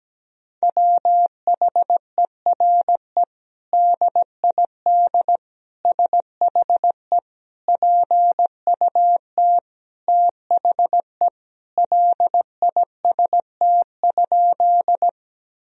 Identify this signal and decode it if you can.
Morse code, heard as a keyed tone, about 17 wpm, 700 Hz. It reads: WHERE DID SHE PUT THE LIST?
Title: Sentences from Top 1000 Words - Review Part 2 - 17wpm